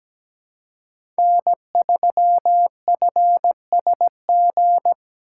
Morse code, heard as a keyed tone, about 17 wpm, 700 Hz. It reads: N3FSG
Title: USA Call Signs - 2x Character Spacing - 17wpm